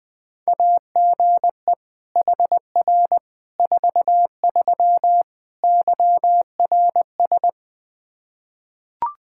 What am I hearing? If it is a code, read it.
Morse code, heard as a keyed tone, about 21 wpm, 700 Hz. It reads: AGE HR 43 YRS
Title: Practice QSO - Exchange 3 - More Info - 20wpm